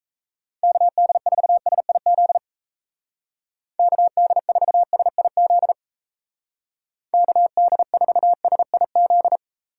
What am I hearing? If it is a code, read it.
Morse code, heard as a keyed tone, about 35 wpm, 700 Hz. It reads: KD4SIZ KD4SIZ KD4SIZ